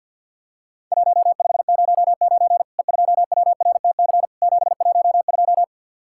Morse code, heard as a keyed tone, about 47 wpm, 700 Hz. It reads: 1509 E2WRTC 712
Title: Worked All Europe DX Contest - QTC Exchange with Repeats - 50wpm